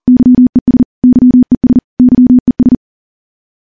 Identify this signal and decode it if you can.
Morse code, heard as a keyed tone, about 40 wpm, 250 Hz. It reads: YES YES YES
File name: morse / morse-training-40WPM